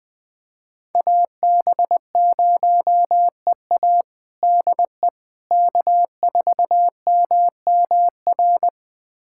Morse code, heard as a keyed tone, about 21 wpm, 700 Hz. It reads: AB0EA DE K4MMR
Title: USA Callsign DE Callsign - 5 Characters - 20wpm